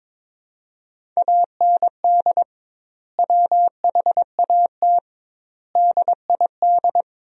Morse code, heard as a keyed tone, about 22 wpm, 700 Hz. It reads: AND WHAT DID
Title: 2x Word-Spacing - Sentences from Top 100 Words - 22wpm